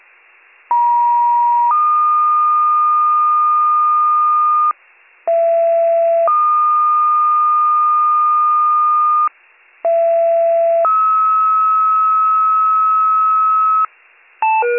Two-tone page, heard as a one-second tone, then a three-second tone, about 950 then 1220 Hz, about 670 then 1220 Hz, about 670 then 1290 Hz. It is heard from 0.7 to 13.9 s.